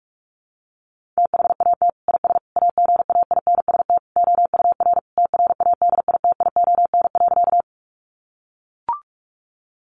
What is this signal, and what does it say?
Morse code, heard as a keyed tone, about 44 wpm, 700 Hz. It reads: THAT IS AGAINST OUR TRADITION.
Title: Sentences from Top 1500 Words 45wpm